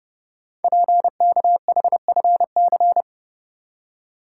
Morse code, heard as a keyed tone, about 30 wpm, 700 Hz. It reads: PKHFC
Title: Code Groups - 5 Letters - 30wpm